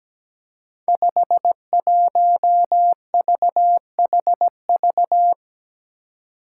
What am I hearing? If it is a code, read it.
Morse code, heard as a keyed tone, about 17 wpm, 700 Hz. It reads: 51VHV
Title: Five Challenging Letter-Number Combinations 17wpm